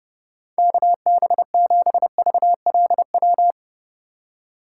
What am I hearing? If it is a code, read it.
Morse code, heard as a keyed tone, about 30 wpm, 700 Hz. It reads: KB7VLW
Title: USA Call Signs - 6 Characters - 4x Character Spacing - 30wpm